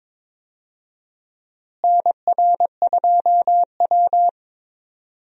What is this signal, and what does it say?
Morse code, heard as a keyed tone, about 21 wpm, 700 Hz. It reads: NR2W
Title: USA Call Signs - 4 Characters - Repeated 3x - 22wpm